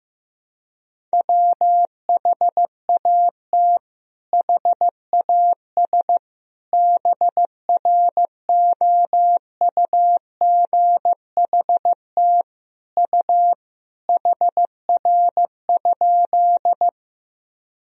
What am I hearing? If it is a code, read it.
Morse code, heard as a keyed tone, about 15 wpm, 700 Hz. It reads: WHAT HAS BROUGHT U HR?
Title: Sentences from Top 500 Words and Abbreviations 15wpm